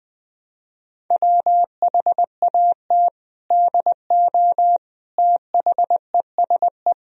Morse code, heard as a keyed tone, about 21 wpm, 700 Hz. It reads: WHAT DO THESE
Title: Sentences from Top 1000 Words - Review Part 1 - 20wpm